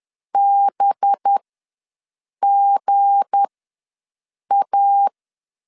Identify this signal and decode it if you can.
Morse code, heard as a keyed tone, about 10 wpm, spaced as for 6 wpm, 800 Hz. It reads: BGA
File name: LEZIONI / 8 YJ45 / 8 YJ ws 10wpm 4043